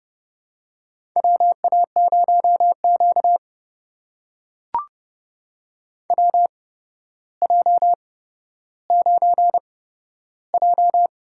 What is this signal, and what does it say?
Morse code, heard as a keyed tone, about 28 wpm, spaced as for 13 wpm, 700 Hz. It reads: WA0Q WJ9J
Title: USA Call Signs - 4 Characters - 8x Character Spacing - 30wpm